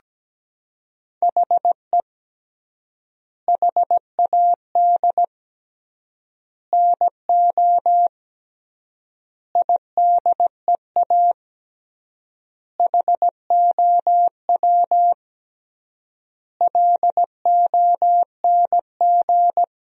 Morse code, heard as a keyed tone, about 17 wpm, 700 Hz. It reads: HE HAD NO IDEA HOW LONG
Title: 3x Word-Spacing - Sentences from Top 300 Words - 17wpm